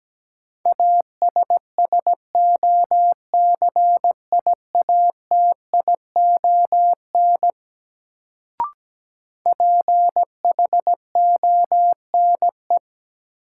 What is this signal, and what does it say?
Morse code, heard as a keyed tone, about 17 wpm, 700 Hz. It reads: ASSOCIATION PHONE
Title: Sentences from Top 1100 Words - New Word Drill - 17wpm